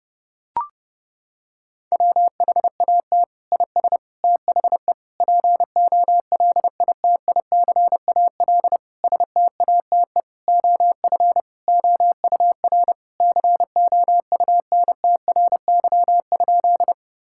Morse code, heard as a keyed tone, about 30 wpm, 700 Hz. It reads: WHAT IS THE POLITICAL STATE OF OUR COUNTRY?
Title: Sentences from Top 1000 Words - Review Part 5 - 30wpm